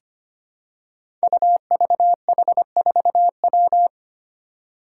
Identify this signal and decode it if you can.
Morse code, heard as a keyed tone, about 25 wpm, 700 Hz. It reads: UVH4W
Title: Five Challenging Letter-Number Combinations 25wpm